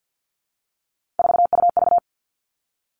Morse code, heard as a keyed tone, about 50 wpm, 700 Hz. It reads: 4UV
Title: Three Challenging Letter-Number Combinations 50wpm